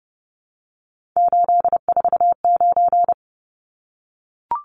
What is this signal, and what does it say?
Morse code, heard as a keyed tone, about 30 wpm, 700 Hz. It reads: 849
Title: Numbers - 3-Digit 30wpm